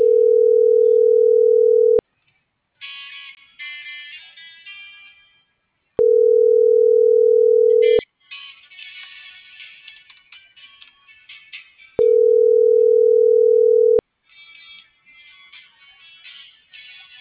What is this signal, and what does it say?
Ringback tone (North America), 440 + 480 Hz.